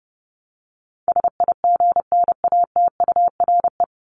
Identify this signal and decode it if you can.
Morse code, heard as a keyed tone, about 30 wpm, 700 Hz. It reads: SIGNATURE
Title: Top 1751-2000 Words 30wpm